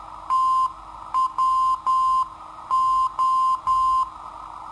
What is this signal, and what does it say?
Morse code, heard as a keyed tone, about 10 wpm, 1040 Hz. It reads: TWO
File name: sitoariold / beacon hk